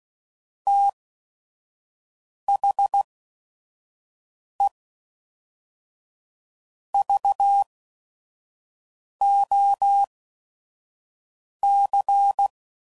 Morse code, heard as a keyed tone, about 16 wpm, spaced as for 5 wpm, 800 Hz. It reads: THE VOC
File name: CW Code Groups @5WPM / QSOs / WD0FBT 1